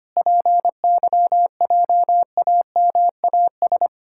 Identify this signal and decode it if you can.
Morse code, heard as a keyed tone, about 25 wpm, 700 Hz. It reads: PYJAMAS
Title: Pangram Challenge v2 - 25wpm